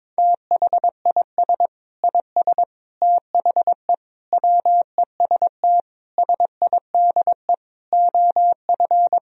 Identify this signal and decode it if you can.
Morse code, heard as a keyed tone, about 22 wpm, 700 Hz. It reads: THIS IS THE WEST SIDE OF
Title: Sentences from Top 1000 Words - Review Part 2 - 22wpm